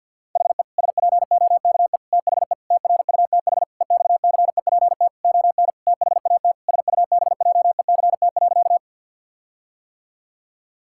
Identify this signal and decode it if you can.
Morse code, heard as a keyed tone, about 51 wpm, 700 Hz. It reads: HE SPOKE THE TRUTH EXCEPT ON THAT SUBJECT.
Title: Sentences from Top 1600 Words 50wpm